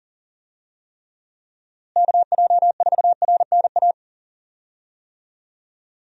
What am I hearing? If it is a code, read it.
Morse code, heard as a keyed tone, about 40 wpm, 700 Hz. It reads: KJ4RNA
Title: USA Call Signs - 6 Characters - Repeated 2x - 40wpm